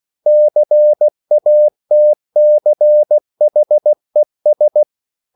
Morse code, heard as a keyed tone, about 16 wpm, 600 Hz. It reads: CATCHES
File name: CW/Practice